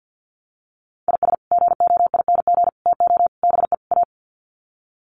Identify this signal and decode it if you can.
Morse code, heard as a keyed tone, about 50 wpm, 700 Hz. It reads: IS GOING TO BE A
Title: Sets of 5 Words 50wpm